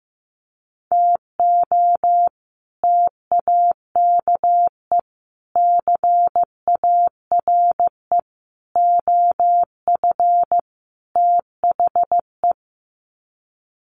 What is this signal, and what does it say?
Morse code, heard as a keyed tone, about 15 wpm, 700 Hz. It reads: TO TAKE CARE OF THE
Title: Sets of 5 Words Encore 15wpm